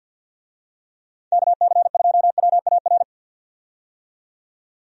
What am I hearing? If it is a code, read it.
Morse code, heard as a keyed tone, about 52 wpm, 700 Hz. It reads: KC2WAR
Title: USA Call Signs - 6 Characters - 50wpm